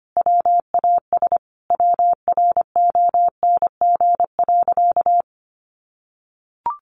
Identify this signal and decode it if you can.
Morse code, heard as a keyed tone, about 25 wpm, 700 Hz. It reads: WAS WRONG.
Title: Sentences from Top 1300 Words 25wpm